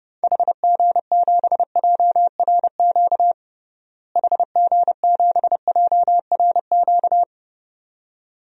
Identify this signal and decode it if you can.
Morse code, heard as a keyed tone, about 32 wpm, 700 Hz. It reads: HG7JRQ HG7JRQ
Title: DXCC-100 Call Signs - 6 Characters - Repeated 2x - 30wpm